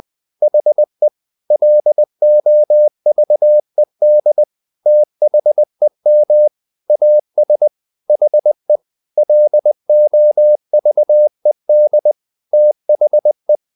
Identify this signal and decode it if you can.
Morse code, heard as a keyed tone, about 20 wpm, 600 Hz. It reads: HE LOVED THEM AS HE LOVED THE
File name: the old man and the sea / the old man and the sea part 1 20wpm